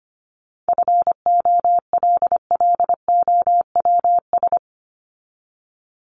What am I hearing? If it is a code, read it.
Morse code, heard as a keyed tone, about 25 wpm, 700 Hz. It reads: FOLLOWS
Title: ES:S Suffix - Top 100 Words with Base Words 25wpm